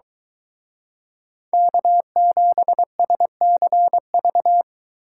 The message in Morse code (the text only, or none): K7SCV